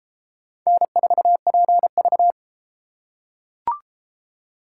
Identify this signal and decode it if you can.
Morse code, heard as a keyed tone, about 31 wpm, 700 Hz. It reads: N4PV